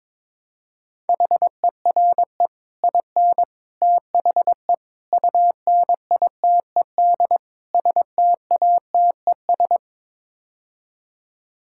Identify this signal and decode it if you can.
Morse code, heard as a keyed tone, about 22 wpm, 700 Hz. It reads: HERE IN THE UNITED STATES
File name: practice / 22wpm / Build A Phrase - 5-Word Phrases - 22wpm